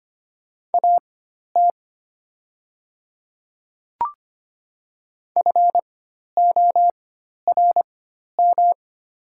Morse code, heard as a keyed tone, about 25 wpm, spaced as for 12 wpm, 690 Hz. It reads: AT FORM
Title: 4x Character Spacing - Top 200 Words - 25wpm